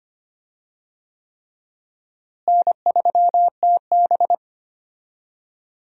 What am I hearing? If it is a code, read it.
Morse code, heard as a keyed tone, about 25 wpm, 700 Hz. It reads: N3TB
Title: USA Call Signs - 4 Characters - Repeated 3x - 25wpm